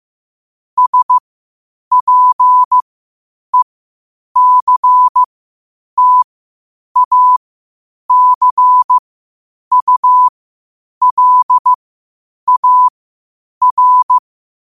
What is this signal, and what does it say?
Morse code, heard as a keyed tone, about 15 wpm, spaced as for 8 wpm, 1000 Hz. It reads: SPECTACULAR
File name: qotdfiles / 8wpm